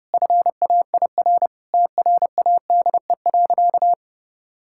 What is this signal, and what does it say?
Morse code, heard as a keyed tone, about 30 wpm, 700 Hz. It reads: FAIR TRADE.
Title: Sentences from Top 1600 Words 30wpm